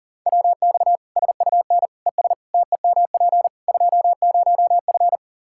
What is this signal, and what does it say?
Morse code, heard as a keyed tone, about 40 wpm, 700 Hz. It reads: WX SUN ES TEMP 20F
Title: Practice QSO - Exchange 2 - Info - 40wpm